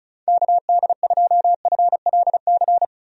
Morse code, heard as a keyed tone, about 36 wpm, 700 Hz. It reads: KD2FLC
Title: USA Call Signs - 6 Characters - Repeated 2x - 35wpm